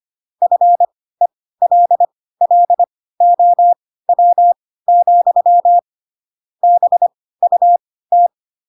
Morse code, heard as a keyed tone, about 25 wpm, spaced as for 16 wpm, 700 Hz. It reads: FELLOW, BUT
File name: Pinocchio / Pinocchio ch3 10 25 ik1hge